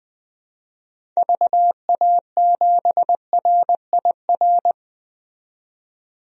Morse code, heard as a keyed tone, about 20 wpm, 700 Hz. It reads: VA7RIR
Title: DXCC-100 Call Signs - 6 Characters - 20wpm